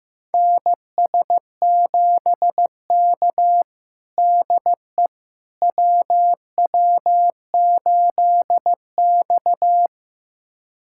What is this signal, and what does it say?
Morse code, heard as a keyed tone, about 15 wpm, 700 Hz. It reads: NS7K DE WW8X
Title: USA Callsign DE Callsign - 4 Characters - 15wpm